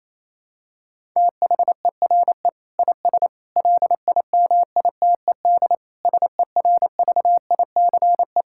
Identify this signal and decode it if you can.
Morse code, heard as a keyed tone, about 28 wpm, 700 Hz. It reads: THERE IS LIMITED SERVICE
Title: Sentences from Top 1000 Words - Review Part 6 - 28wpm